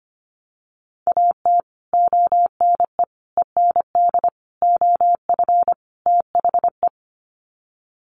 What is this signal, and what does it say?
Morse code, heard as a keyed tone, about 25 wpm, 700 Hz. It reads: AT ONE END OF THE
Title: Sets of 5 Words Encore 25wpm